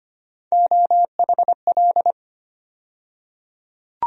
Morse code, heard as a keyed tone, about 25 wpm, 700 Hz. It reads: OHL
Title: USA Call Sign Postfixes - 3 Letters 25wpm